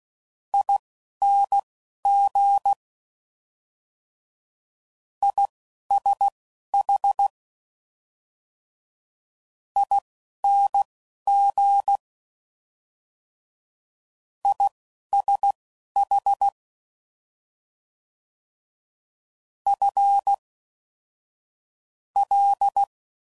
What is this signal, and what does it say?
Morse code, heard as a keyed tone, about 16 wpm, spaced as for 11 wpm, 800 Hz. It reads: ING ISH ING ISH F L